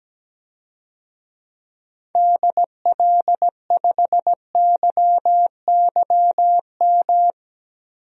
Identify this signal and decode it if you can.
Morse code, heard as a keyed tone, about 17 wpm, 700 Hz. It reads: DL5YYM
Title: DXCC-100 Call Signs - 6 Characters - 17wpm